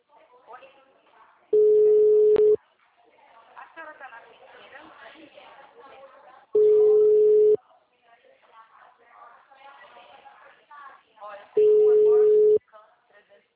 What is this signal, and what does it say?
Ringback tone (Europe, 425 Hz), with speech.
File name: out- -2012- - - .55982